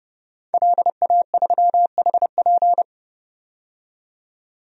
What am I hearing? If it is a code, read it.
Morse code, heard as a keyed tone, about 30 wpm, 700 Hz. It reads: LA3HP
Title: DXCC-100 Call Signs - 5 Characters - 30wpm